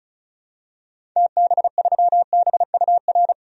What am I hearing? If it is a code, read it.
Morse code, heard as a keyed tone, about 36 wpm, 700 Hz. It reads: TB3BUR
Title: DXCC-100 Call Signs - 6 Characters - 35wpm